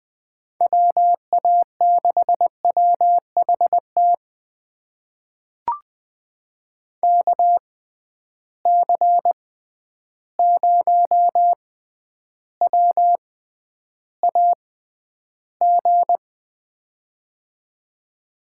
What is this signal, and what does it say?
Morse code, heard as a keyed tone, about 20 wpm, spaced as for 10 wpm, 700 Hz. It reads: WA6WHT KC0WAG
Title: USA Call Signs - 6 Characters - 6x Character Spacing - 20wpm